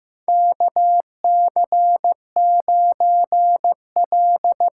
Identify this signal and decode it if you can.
Morse code, heard as a keyed tone, about 15 wpm, 700 Hz. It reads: KC9L
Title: USA Call Signs - 6 Characters - 8x Character Spacing - 15wpm